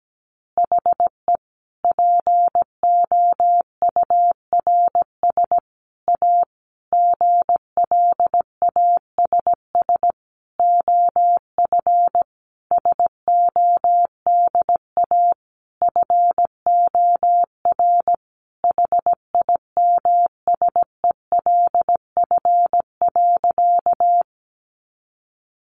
Morse code, heard as a keyed tone, about 17 wpm, 700 Hz. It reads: HE POURS A GLASS OF SODA FOR HIMSELF.
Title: Additional Easy Phrases 17wpm